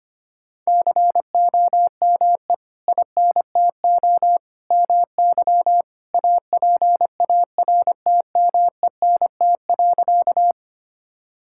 Morse code, heard as a keyed tone, about 25 wpm, 700 Hz. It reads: COME INTO MY APARTMENT.
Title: Sentences from Top 1000 Words - Review Part 6 - 25wpm